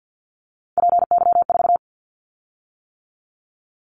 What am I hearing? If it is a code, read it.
Morse code, heard as a keyed tone, about 50 wpm, 700 Hz. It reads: PY4
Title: Callsign Prefixes for Top 100 DXCC - 50wpm